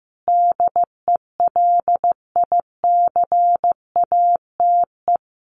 Morse code, heard as a keyed tone, about 14 wpm, 700 Hz. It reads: DELICATE
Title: A Sound Of Thunder 15wpm